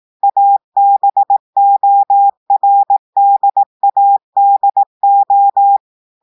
Morse code, heard as a keyed tone, about 18 wpm, 800 Hz. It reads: ABORDADO
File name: audios / portugais / portugais texte en clair 2 vitesse 18 mots minutes